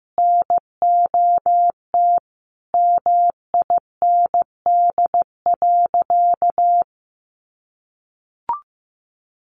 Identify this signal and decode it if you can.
Morse code, heard as a keyed tone, about 14 wpm, 700 Hz. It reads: NOT MIND.
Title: Sentences from Top 600 Words 15wpm